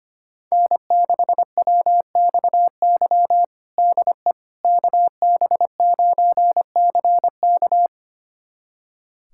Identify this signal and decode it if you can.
Morse code, heard as a keyed tone, about 25 wpm, 700 Hz. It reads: N6WXY DE KB9CK
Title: USA Callsign DE Callsign - 5 Characters - 25wpm